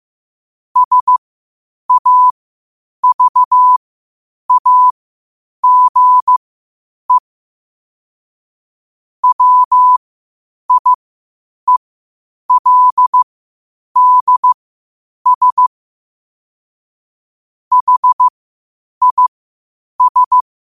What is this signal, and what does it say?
Morse code, heard as a keyed tone, about 15 wpm, spaced as for 8 wpm, 1000 Hz. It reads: SAVAGE WIELDS HIS
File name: qotdfiles / 8wpm